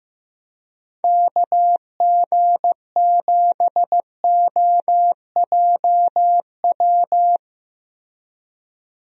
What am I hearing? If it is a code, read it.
Morse code, heard as a keyed tone, about 15 wpm, 700 Hz. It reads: KG7OJW
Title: USA Call Signs - 6 Characters - Repeated 2x - 15wpm